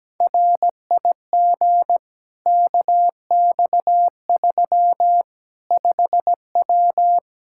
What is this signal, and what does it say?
Morse code, heard as a keyed tone, about 17 wpm, 700 Hz. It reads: RIG KX3 5W
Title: Practice QSO - Exchange 2 - Info - 17wpm